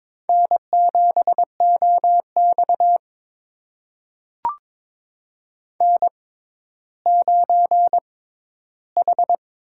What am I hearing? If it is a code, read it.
Morse code, heard as a keyed tone, about 23 wpm, spaced as for 13 wpm, 700 Hz. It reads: N7OX N9H
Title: USA Call Signs - 6x Character Spacing - 22wpm